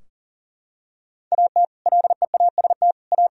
Morse code, heard as a keyed tone, about 37 wpm, 700 Hz. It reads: AT LEAST A
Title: Sentences from Top 1000 Words - Review Part 2 - 40wpm